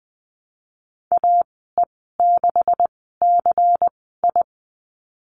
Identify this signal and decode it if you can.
Morse code, heard as a keyed tone, about 20 wpm, spaced as for 15 wpm, 700 Hz. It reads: AE6CI
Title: 179 Lesson 29g - Slash - Call Signs 20wpm